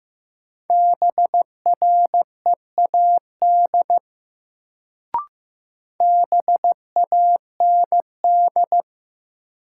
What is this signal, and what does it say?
Morse code, heard as a keyed tone, about 15 wpm, 700 Hz. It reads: BREAD BAND
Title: Sentences from Top 1600 Words - New Word Drill - 15wpm